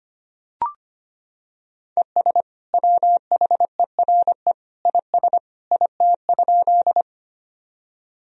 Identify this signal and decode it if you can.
Morse code, heard as a keyed tone, about 25 wpm, 700 Hz. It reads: ES WHERE IS IT?
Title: Sentences from Top 200 Words and Abbreviations 25wpm